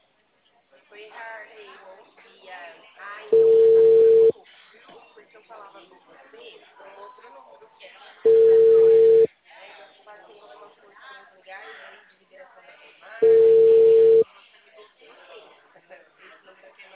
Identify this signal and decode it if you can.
Ringback tone (China, 450 Hz), with speech.
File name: out- -2006- - - .25